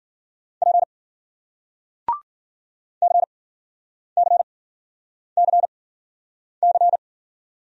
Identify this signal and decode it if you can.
Morse code, heard as a keyed tone, about 52 wpm, spaced as for 9 wpm, 700 Hz. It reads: P CCCC